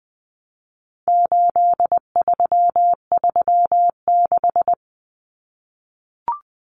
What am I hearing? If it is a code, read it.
Morse code, heard as a keyed tone, about 20 wpm, 700 Hz. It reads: 8336
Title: Numbers - 4-Digit 20wpm